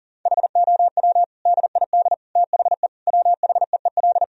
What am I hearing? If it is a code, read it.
Morse code, heard as a keyed tone, about 40 wpm, 700 Hz. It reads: HOW DID THE WHEEL
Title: Sentences from Top 1000 Words - Review Part 3 - 40wpm